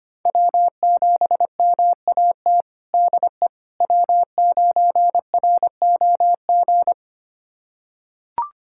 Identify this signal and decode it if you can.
Morse code, heard as a keyed tone, about 25 wpm, 700 Hz. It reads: W7MAT DE W9ROG
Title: USA Callsign DE Callsign - 5 Characters - 25wpm